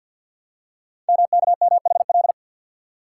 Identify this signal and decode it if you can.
Morse code, heard as a keyed tone, about 48 wpm, 700 Hz. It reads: MKMHL
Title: Code Groups - 5 Letters - 50wpm